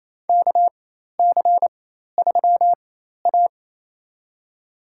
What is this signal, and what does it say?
Morse code, heard as a keyed tone, about 28 wpm, spaced as for 13 wpm, 700 Hz. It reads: KC3A
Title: USA Call Signs - 4x Character Spacing - 28wpm